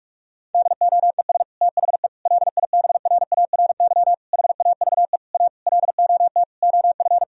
Morse code, heard as a keyed tone, about 46 wpm, 700 Hz. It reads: DOES THE LIBRARY HAVE A LOT OF
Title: Sentences from Top 1000 Words - Review Part 4 - 45wpm